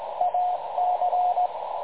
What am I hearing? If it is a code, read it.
Morse code, heard as a keyed tone, about 18 wpm, 720 Hz, with radio noise. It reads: AC
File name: dlazaawansowanych / nagraniazpasma / websdr recording 3565.0kHz